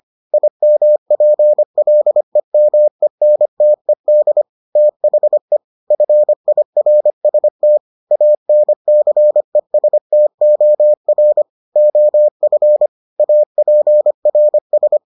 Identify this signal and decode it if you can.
Morse code, heard as a keyed tone, about 25 wpm, 600 Hz. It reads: IMPLEMENTED THE FIRST ANCESTOR OF APRS